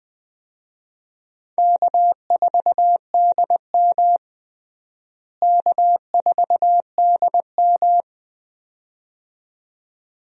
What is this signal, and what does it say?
Morse code, heard as a keyed tone, about 21 wpm, 700 Hz. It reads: K4DM K4DM
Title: USA Call Signs - 4 Characters - Repeated 2x - 20wpm